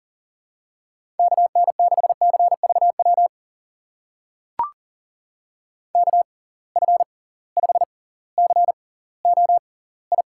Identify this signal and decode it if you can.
Morse code, heard as a keyed tone, about 40 wpm, spaced as for 20 wpm, 700 Hz. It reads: KN6CVW KF5COI
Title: USA Call Signs - 6 Characters - 6x Character Spacing - 40wpm